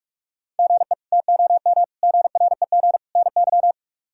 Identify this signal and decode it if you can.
Morse code, heard as a keyed tone, about 45 wpm, 700 Hz. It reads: GE TOM GREG NY
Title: K1USN SST QSO Practice - 45wpm